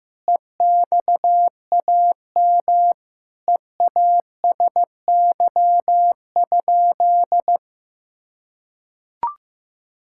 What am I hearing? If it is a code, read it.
Morse code, heard as a keyed tone, about 15 wpm, 700 Hz. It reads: EXAM EASY?
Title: Sentences from Top 1000 Words - Review Part 5 - 15wpm